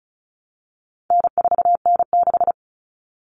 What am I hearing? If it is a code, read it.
Morse code, heard as a keyed tone, about 34 wpm, 700 Hz. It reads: N4N6
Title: Common Cut Numbers - 4-Digit 35wpm